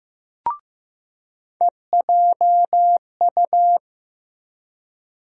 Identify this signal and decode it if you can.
Morse code, heard as a keyed tone, about 15 wpm, 700 Hz. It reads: EJU
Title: USA Call Sign Postfixes - 3 Letters 15wpm